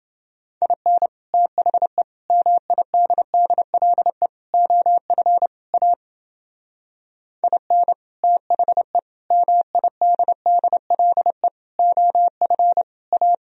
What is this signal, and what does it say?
Morse code, heard as a keyed tone, about 32 wpm, 700 Hz. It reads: IN THE MIDDLE OF A IN THE MIDDLE OF A